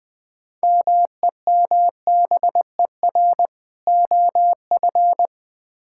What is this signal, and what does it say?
Morse code, heard as a keyed tone, about 20 wpm, 700 Hz. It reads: MEMBER OF
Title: Build A Phrase - 3-Word Phrases - 20wpm